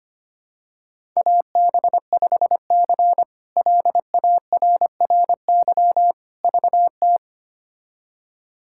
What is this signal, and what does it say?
Morse code, heard as a keyed tone, about 25 wpm, 700 Hz. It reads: AB5C LARRY VT
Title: K1USN SST QSO Practice - 25wpm